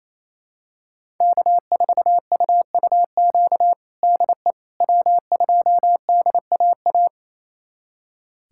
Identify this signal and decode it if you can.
Morse code, heard as a keyed tone, about 28 wpm, 700 Hz. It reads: K4UUQ DE W2DAA
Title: USA Callsign DE Callsign - 5 Characters - 28wpm